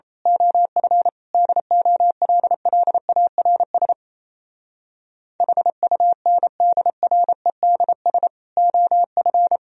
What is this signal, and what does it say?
Morse code, heard as a keyed tone, about 27 wpm, 700 Hz. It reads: OF DOLLARS HUNDREDS OF
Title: Speed-Racing - Sets of 5 Words 28wpm